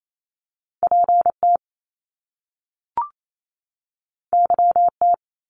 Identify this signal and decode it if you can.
Morse code, heard as a keyed tone, about 28 wpm, 700 Hz. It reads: PT YT
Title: Top Two Letter Combinations in Words 28wpm